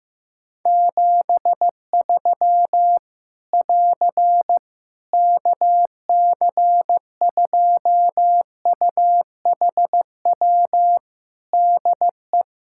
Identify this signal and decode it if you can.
Morse code, heard as a keyed tone, about 15 wpm, 700 Hz. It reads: 73 <AR> KC2UHW DE
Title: Practice QSO - Exchange 4 - Ending - 15wpm